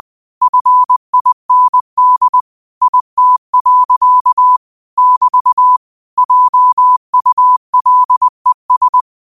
Morse code, heard as a keyed tone, about 20 wpm, 1000 Hz. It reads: FIND IT. <BT> JULES